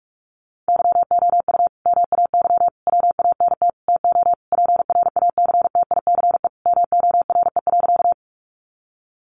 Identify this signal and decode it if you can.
Morse code, heard as a keyed tone, about 44 wpm, 700 Hz. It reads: YOU MAY WANT TO PRACTICE MORE.
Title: Sentences from Top 700 Words 45wpm